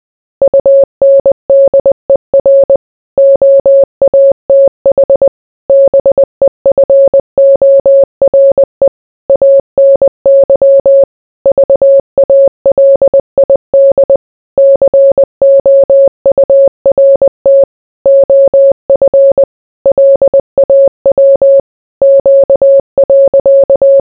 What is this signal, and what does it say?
Morse code, heard as a keyed tone, about 20 wpm, 550 Hz. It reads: UNDER OATH BEFORE ANY VALID COURT OF LAW Q.